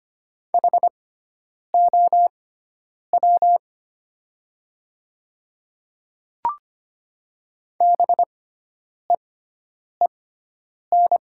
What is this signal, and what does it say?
Morse code, heard as a keyed tone, about 25 wpm, spaced as for 9 wpm, 700 Hz. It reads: HOW BEEN
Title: 6x Character Spacing - Top 100 Words - 25wpm